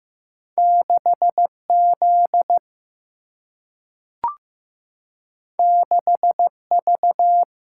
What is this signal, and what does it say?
Morse code, heard as a keyed tone, about 15 wpm, 700 Hz. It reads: 6 Z 6 V
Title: Single-Letter Single-Number 15wpm